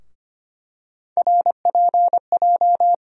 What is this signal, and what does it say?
Morse code, heard as a keyed tone, about 25 wpm, 700 Hz. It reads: RPJ
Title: USA Call Sign Postfixes - 3 Letters 25wpm